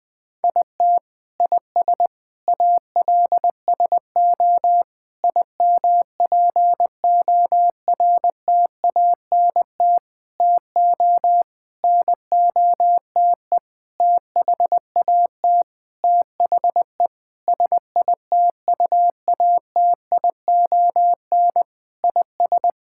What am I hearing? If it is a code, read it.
Morse code, heard as a keyed tone, about 21 wpm, 700 Hz. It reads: IT IS ALSO IMPORTANT TO NOTE THAT THE SITUATION IS
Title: Sentences with 8-Word Phrases - Spoken Beginning - 20wpm